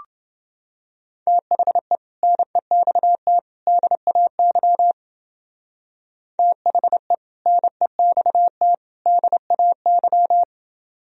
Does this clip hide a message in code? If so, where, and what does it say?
Yes: Morse code from 1.3 to 10.4 s, heard as a keyed tone, about 29 wpm, 700 Hz. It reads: THE NEXT DAY THE NEXT DAY